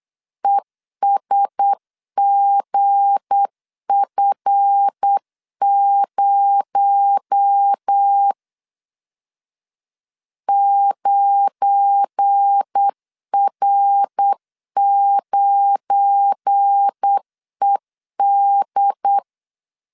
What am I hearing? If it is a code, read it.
Morse code, heard as a keyed tone, about 8 wpm, 800 Hz. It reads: ESGF0 9R9ED
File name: LEZIONI / 7 GF9Z / 7 GF9 8wpm 3545